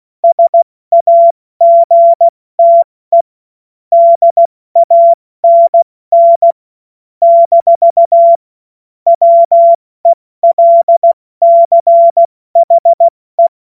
Morse code, heard as a keyed tone, about 16 wpm, spaced as for 14 wpm, 680 Hz. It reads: SAGTE DANN - WELCHE
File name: Morsekurs / Witze / Witz 09 Januar 2024 70 F80